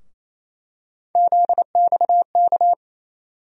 Morse code, heard as a keyed tone, about 28 wpm, 700 Hz. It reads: ZXK
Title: USA Call Sign Postfixes - 3 Letters 28wpm